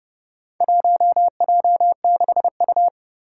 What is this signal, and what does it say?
Morse code, heard as a keyed tone, about 28 wpm, 700 Hz. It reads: 1J6U
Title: Four Challenging Letter-Number Combinations 30wpm